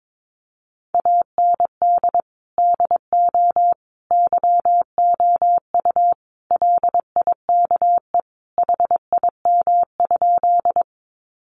Morse code, heard as a keyed tone, about 22 wpm, 700 Hz. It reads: AND DO YOU LIKE HIM?
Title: Sentences from Top 100 Words 22wpm